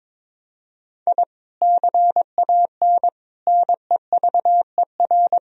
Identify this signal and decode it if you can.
Morse code, heard as a keyed tone, about 22 wpm, 700 Hz. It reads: I CAN NEVER
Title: Sentences from Top 1000 Words - Review Part 3 - 22wpm